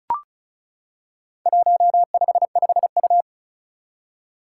Morse code, heard as a keyed tone, about 36 wpm, 700 Hz. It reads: 155U
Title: Four Challenging Letter-Number Combinations 35wpm